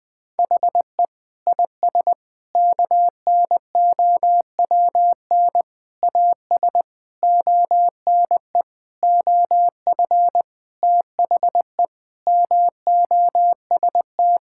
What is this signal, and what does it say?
Morse code, heard as a keyed tone, about 21 wpm, 700 Hz. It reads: HE IS KNOWN AS ONE OF THE MOST
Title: Sentences with 5-Word Phrases - 20wpm